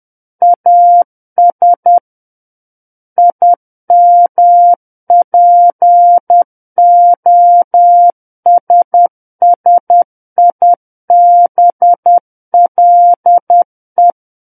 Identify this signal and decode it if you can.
Morse code, heard as a keyed tone, about 10 wpm, 700 Hz. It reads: AS IMPOSSIBLE
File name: Book One / 01-Chapter 1 - 10 WPM